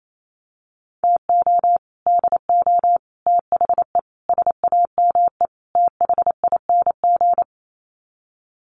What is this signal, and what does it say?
Morse code, heard as a keyed tone, about 28 wpm, 700 Hz. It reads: TO DO THE SAME THING
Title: Sets of 5 Words Encore 28wpm